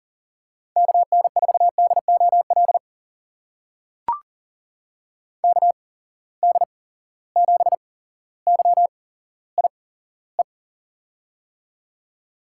Morse code, heard as a keyed tone, about 39 wpm, spaced as for 17 wpm, 700 Hz. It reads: KN4DOL KD7YIE
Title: USA Call Signs - 8x Character Spacing - 40wpm